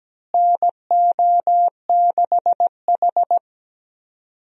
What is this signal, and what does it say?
Morse code, heard as a keyed tone, about 17 wpm, 700 Hz. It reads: NO6H
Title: USA Call Signs - 4 Characters - Repeated 2x - 17wpm